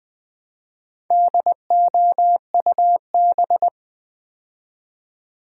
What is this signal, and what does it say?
Morse code, heard as a keyed tone, about 20 wpm, 700 Hz. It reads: DOUB
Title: Build A Word - 2 Letters At A Time - 6-Letter Words - 20wpm